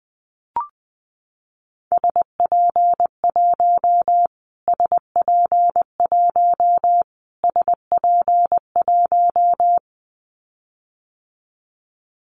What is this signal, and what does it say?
Morse code, heard as a keyed tone, about 20 wpm, 700 Hz. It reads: SP1 SP1 SP1
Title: Callsign Prefixes for Top 25 DXCC - Repeated 3x - 20wpm